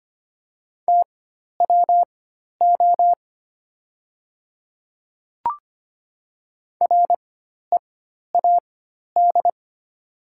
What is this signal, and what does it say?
Morse code, heard as a keyed tone, about 25 wpm, spaced as for 12 wpm, 700 Hz. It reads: TWO READ
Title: 4x Character Spacing - Top 200 Words - 25wpm